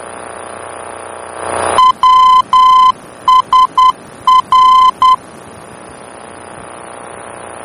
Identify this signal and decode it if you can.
Morse code, heard as a keyed tone, about 10 wpm, 1020 Hz. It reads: WSR
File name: germany / mx / WSR-D-VOR